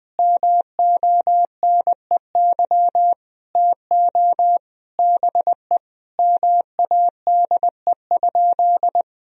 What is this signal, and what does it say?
Morse code, heard as a keyed tone, about 21 wpm, 700 Hz. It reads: MONEY TO BE MADE?
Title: Sentences from Top 1000 Words - Review Part 4 - 20wpm